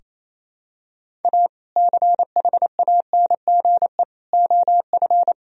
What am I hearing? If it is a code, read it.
Morse code, heard as a keyed tone, about 28 wpm, 700 Hz. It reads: A CHANGE OF
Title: Sentences from Top 1000 Words - Review Part 2 - 28wpm